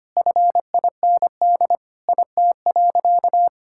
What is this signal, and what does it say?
Morse code, heard as a keyed tone, about 25 wpm, 700 Hz. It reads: FIND IT.
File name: practice / 25wpm / Sentences with 3-Word Phrases - 25wpm